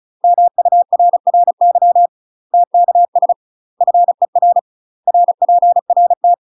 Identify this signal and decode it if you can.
Morse code, heard as a keyed tone, about 35 wpm, 700 Hz. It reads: MURRY TKS FER RPRT
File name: UK MASTER QSOs TRAINING / 35WPM / UK MASTER QSOs 0002